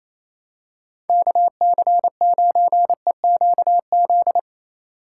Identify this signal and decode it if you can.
Morse code, heard as a keyed tone, about 28 wpm, 700 Hz. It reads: KC9EQZ
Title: USA Call Signs - 3x Character Spacing - 28wpm